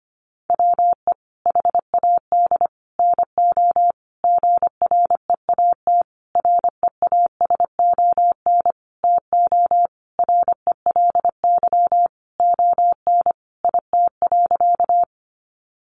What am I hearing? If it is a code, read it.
Morse code, heard as a keyed tone, about 25 wpm, 700 Hz. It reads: WE HAD NO GREAT REASON TO RELY ON IT.
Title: Sentences from Top 1200 Words 25wpm